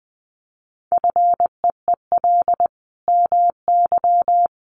Morse code, heard as a keyed tone, about 20 wpm, 700 Hz. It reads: FEEL MY
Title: Sentences from Top 800 Words 20wpm